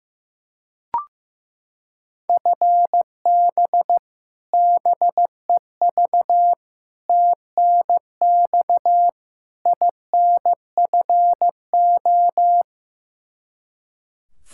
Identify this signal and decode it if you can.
Morse code, heard as a keyed tone, about 15 wpm, 700 Hz. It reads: FB BEV TNX INFO
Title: Practice QSO - Exchange 3 - More Info - 15wpm